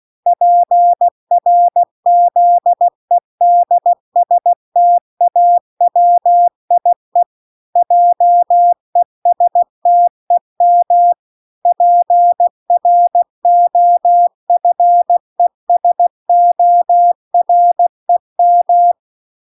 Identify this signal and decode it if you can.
Morse code, heard as a keyed tone, about 16 wpm, 700 Hz. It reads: PRZEDSTAWIE JESTEM PROFESOREM